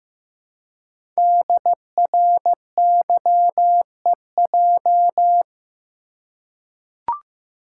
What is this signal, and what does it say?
Morse code, heard as a keyed tone, about 15 wpm, 700 Hz. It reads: DRYEJ
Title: Code Groups - 5 Letters - 15wpm